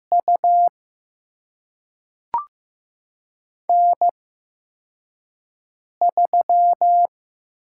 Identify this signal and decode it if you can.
Morse code, heard as a keyed tone, about 15 wpm, spaced as for 4 wpm, 700 Hz. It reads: U N3
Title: USA Call Signs - 5 Characters - 8x Character Spacing - 15wpm